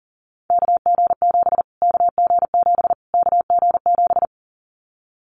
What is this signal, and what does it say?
Morse code, heard as a keyed tone, about 39 wpm, 700 Hz. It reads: KG7 KG7 KG7
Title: Callsign Prefixes for Top 25 DXCC - Repeated 3x - 40wpm